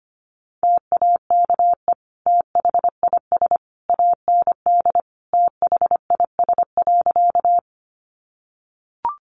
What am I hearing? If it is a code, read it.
Morse code, heard as a keyed tone, about 25 wpm, 700 Hz. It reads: TAKE THIS AND THIS.
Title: Sentences from Top 200 Words 25wpm